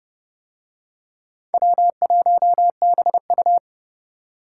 Morse code, heard as a keyed tone, about 31 wpm, 690 Hz. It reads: W1BU